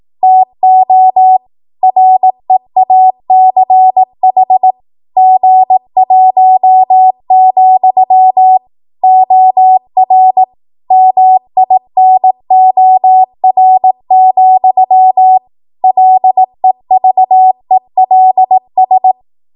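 Morse code, heard as a keyed tone, about 18 wpm, 750 Hz. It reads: TO REACH G1, OR MINOR, LEVELS
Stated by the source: Code practice at 18 WPM, a transition run bridging 15 to 20 WPM.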